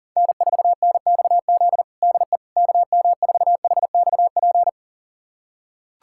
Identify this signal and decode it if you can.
Morse code, heard as a keyed tone, about 40 wpm, 700 Hz. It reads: N4NXZ DE KM4HXP
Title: USA Callsign DE Callsign 40wpm